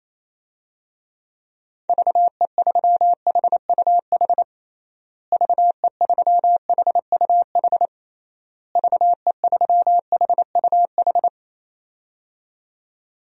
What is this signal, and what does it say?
Morse code, heard as a keyed tone, about 28 wpm, 700 Hz. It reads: VE3HUH VE3HUH VE3HUH
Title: DXCC-100 Call Signs - 6 Characters - Repeated 3x - 28wpm